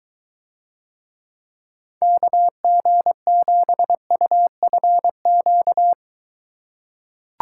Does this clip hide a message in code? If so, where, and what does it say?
Yes: Morse code from 2.0 to 5.9 s, heard as a keyed tone, about 23 wpm, 700 Hz. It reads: KG7UFQ